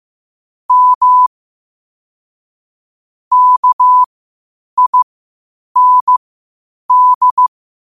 Morse code, heard as a keyed tone, about 15 wpm, spaced as for 8 wpm, 1000 Hz. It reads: M KIND
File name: qotdfiles / 8wpm